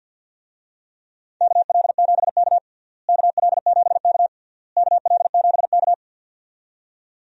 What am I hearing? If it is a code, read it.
Morse code, heard as a keyed tone, about 50 wpm, 700 Hz. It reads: KL7K KL7K KL7K
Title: DXCC-100 Call Signs - 4 Characters - Repeated 3x - 50wpm